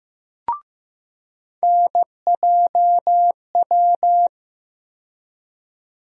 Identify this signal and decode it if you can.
Morse code, heard as a keyed tone, about 15 wpm, 700 Hz. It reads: NJW
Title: USA Call Sign Postfixes - 3 Letters 15wpm